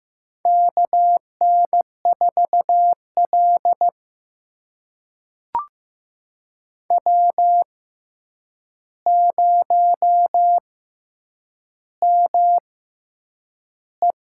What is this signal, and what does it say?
Morse code, heard as a keyed tone, about 15 wpm, spaced as for 8 wpm, 700 Hz. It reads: KN4L W0ME
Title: USA Call Signs - 4 Characters - 6x Character Spacing - 15wpm